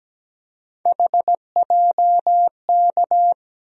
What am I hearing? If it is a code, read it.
Morse code, heard as a keyed tone, about 17 wpm, 700 Hz. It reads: HJK
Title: USA Call Sign Postfixes - 3 Letters 17wpm